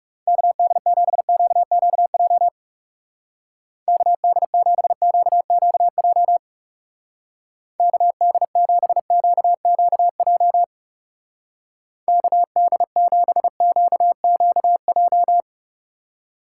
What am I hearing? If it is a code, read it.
Morse code, heard as a keyed tone, about 36 wpm, 700 Hz. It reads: KD7QQJ KD7QQJ KD7QQJ KD7QQJ
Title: Speed Racing - USA Call Signs 30wpm